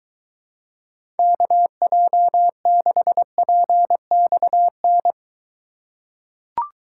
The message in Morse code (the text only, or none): KJ6PXN